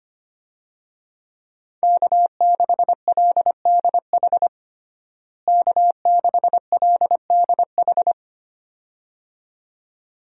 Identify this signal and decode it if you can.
Morse code, heard as a keyed tone, about 25 wpm, 700 Hz. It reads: K6LDH K6LDH
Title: USA Call Signs - 5 Characters - Repeated 2x - 25wpm